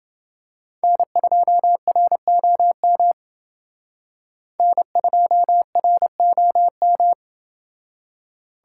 Morse code, heard as a keyed tone, about 28 wpm, 700 Hz. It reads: N2ROM N2ROM